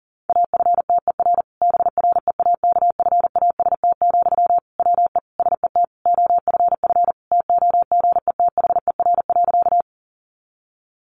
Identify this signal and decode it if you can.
Morse code, heard as a keyed tone, about 40 wpm, 700 Hz. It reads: AFTER BREAKFAST, WE SET OFF TOGETHER.
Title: Sentences from Top 1500 Words 40wpm